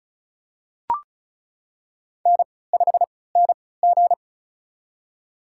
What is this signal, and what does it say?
Morse code, heard as a keyed tone, about 36 wpm, spaced as for 20 wpm, 700 Hz. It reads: N5NG
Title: USA Call Signs - 4 Characters - 3x Character Spacing - 35wpm